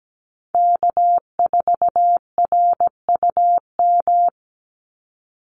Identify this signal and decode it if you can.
Morse code, heard as a keyed tone, about 17 wpm, 700 Hz. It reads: K4RUM
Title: Call Signs Repeated 3x - 17wpm